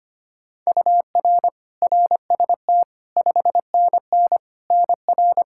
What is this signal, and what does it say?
Morse code, heard as a keyed tone, about 25 wpm, 700 Hz. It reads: UR RST 5NN NR
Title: Practice QSO - Exchange 1 - Beginning - 25wpm